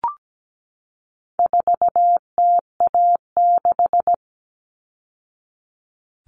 Morse code, heard as a keyed tone, about 17 wpm, 700 Hz. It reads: 4TA6
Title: All Cut Numbers - 4-Digit 17wpm